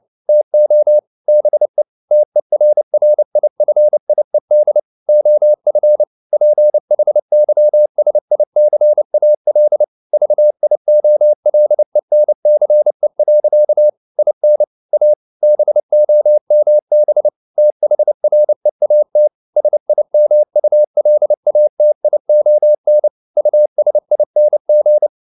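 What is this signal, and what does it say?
Morse code, heard as a keyed tone, about 29 wpm, 600 Hz. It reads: TO BE TERRIFIED OF PHYSICAL VIOLENCE. IN A BOMB THREAT SIMULATION USING
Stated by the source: Morse code at 29 wpm, sending Ken Follett's novel Never in English, from CW med Gnister.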